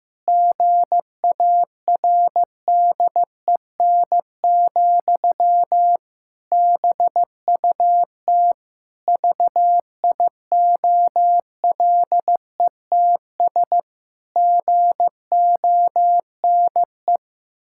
Morse code, heard as a keyed tone, about 15 wpm, 700 Hz. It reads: GARDEN, BUT VIOLETS GONE
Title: Pangram Challenge v2 - 15wpm